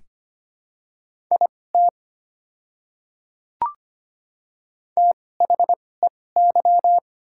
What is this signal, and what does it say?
Morse code, heard as a keyed tone, about 25 wpm, spaced as for 18 wpm, 690 Hz. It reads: IT THEY
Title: 2x Character Spacing - Top 100 Words - 25wpm